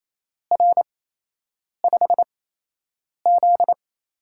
Morse code, heard as a keyed tone, about 28 wpm, 700 Hz. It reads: R 5 Z
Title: USA Call Signs - 8x Character Spacing - 28wpm